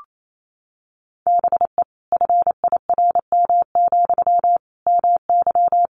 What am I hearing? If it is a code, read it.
Morse code, heard as a keyed tone, about 28 wpm, 700 Hz. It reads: BE FIRM, MY
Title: Sentences from Top 1400 Words 28wpm